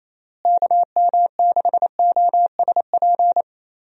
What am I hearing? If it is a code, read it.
Morse code, heard as a keyed tone, about 27 wpm, 700 Hz. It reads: KM6OSP
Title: USA Call Signs - 6 Characters - Repeated 2x - 28wpm